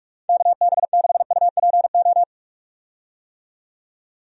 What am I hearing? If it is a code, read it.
Morse code, heard as a keyed tone, about 46 wpm, 700 Hz. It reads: KB6UPO
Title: USA Call Signs - 6 Characters - 45wpm